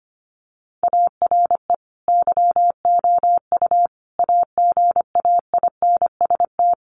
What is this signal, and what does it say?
Morse code, heard as a keyed tone, about 25 wpm, 700 Hz. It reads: ARE YOU AGAINST
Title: Sentences from Top 1400 Words 25wpm